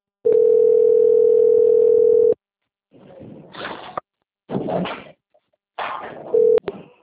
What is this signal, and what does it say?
Ringback tone (North America), 440 + 480 Hz.